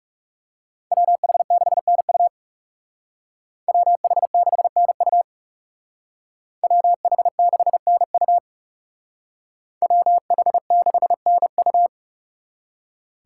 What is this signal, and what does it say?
Morse code, heard as a keyed tone, about 38 wpm, 700 Hz. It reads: WH6NU WH6NU WH6NU WH6NU